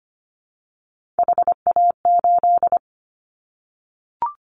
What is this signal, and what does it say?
Morse code, heard as a keyed tone, about 25 wpm, 700 Hz. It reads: HA8
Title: Callsign Prefixes for Top 25 DXCC - 25wpm